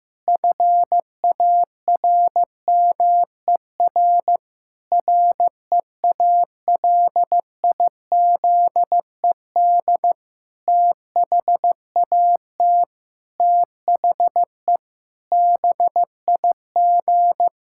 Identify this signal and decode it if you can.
Morse code, heard as a keyed tone, about 15 wpm, 700 Hz. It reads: FARMER REALIZED THAT THE BIG
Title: Pangram Challenge v2 - 15wpm